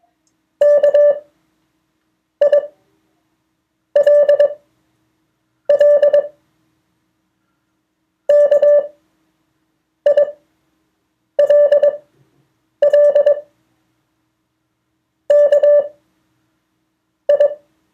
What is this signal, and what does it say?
Morse code, heard as a keyed tone, about 21 wpm, spaced as for 6 wpm, 580 Hz. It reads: KILL KILL KI
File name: 2of4-words-1